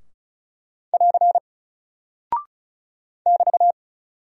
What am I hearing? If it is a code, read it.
Morse code, heard as a keyed tone, about 36 wpm, 700 Hz. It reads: <AR> <BT>
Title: Single Character (Letter:Number:Punctuation:Prosign) 35wpm